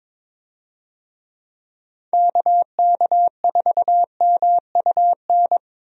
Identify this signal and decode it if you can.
Morse code, heard as a keyed tone, about 22 wpm, 700 Hz. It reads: KK4MUN
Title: USA Call Signs - 6 Characters - Repeated 2x - 22wpm